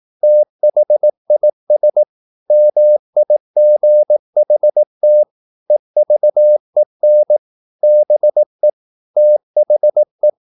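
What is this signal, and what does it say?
Morse code, heard as a keyed tone, about 18 wpm, 600 Hz. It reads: THIS MIGHT EVEN BE THE
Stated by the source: Morse practice at 18 wpm, English text, from CW med Gnister.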